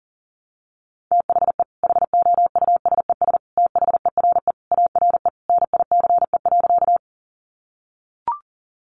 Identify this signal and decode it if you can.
Morse code, heard as a keyed tone, about 40 wpm, 700 Hz. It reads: THE HOUSES THERE ARE NICE.
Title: Easy Phrases 40wpm